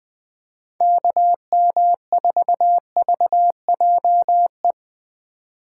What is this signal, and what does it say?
Morse code, heard as a keyed tone, about 20 wpm, 700 Hz. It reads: KM4VJE
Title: USA Call Signs - 4x Character Spacing - 20wpm